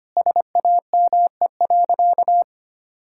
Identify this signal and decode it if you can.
Morse code, heard as a keyed tone, about 25 wpm, 700 Hz. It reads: SAME.
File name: practice / 25wpm / Sentences with 2-Word Phrases - 25wpm